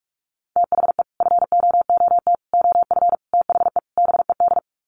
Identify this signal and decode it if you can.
Morse code, heard as a keyed tone, about 44 wpm, 700 Hz. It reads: THE FOOT OF THE BED
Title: Sets of 5 Words Encore2 45wpm